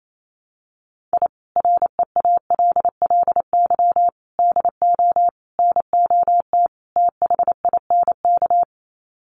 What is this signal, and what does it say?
Morse code, heard as a keyed tone, about 28 wpm, 700 Hz. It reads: I REALLY DO NOT THINK
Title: Sets of 5 Words Encore2 28wpm